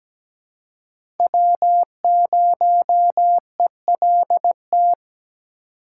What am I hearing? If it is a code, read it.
Morse code, heard as a keyed tone, about 17 wpm, 700 Hz. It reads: W0ELT
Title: USA Call Signs - 5 Characters - 2x Character Spacing - 17wpm